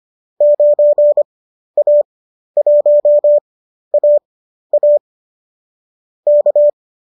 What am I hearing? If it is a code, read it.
Morse code, heard as a keyed tone, about 25 wpm, spaced as for 12 wpm, 590 Hz. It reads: 9A1AA K
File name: qso012 12